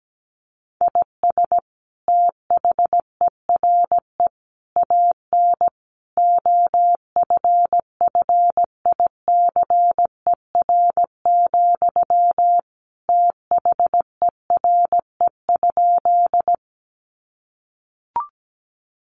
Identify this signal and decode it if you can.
Morse code, heard as a keyed tone, about 17 wpm, 700 Hz. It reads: IS THERE AN OFFICER, THERE?
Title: Sentences from Top 1200 Words 17wpm